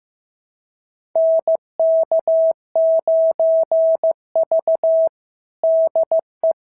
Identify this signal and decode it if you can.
Morse code, heard as a keyed tone, about 14 wpm, 650 Hz. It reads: NK9V DE
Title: SOTA QSO Practice 15wpm